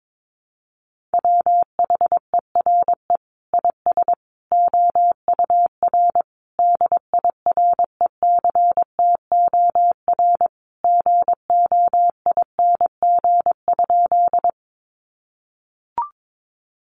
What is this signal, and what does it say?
Morse code, heard as a keyed tone, about 22 wpm, 700 Hz. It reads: WHERE IS OUR DIRECTOR GOING?
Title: Sentences from Top 1300 Words 22wpm